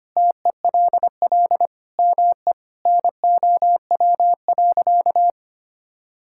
Morse code, heard as a keyed tone, about 25 wpm, 700 Hz. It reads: TELL ME NOW.
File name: practice / 25wpm / Sentences with 4-Word Phrases - 25wpm